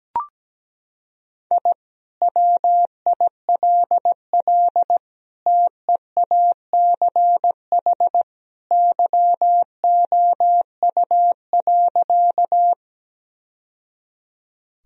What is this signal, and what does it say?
Morse code, heard as a keyed tone, about 17 wpm, 700 Hz. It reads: I WILL TEACH YOU.
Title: Sentences from Top 1000 Words - Review Part 2 - 17wpm